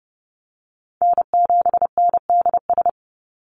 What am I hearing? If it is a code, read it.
Morse code, heard as a keyed tone, about 30 wpm, 700 Hz. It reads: N7NDS
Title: USA Call Signs 30wpm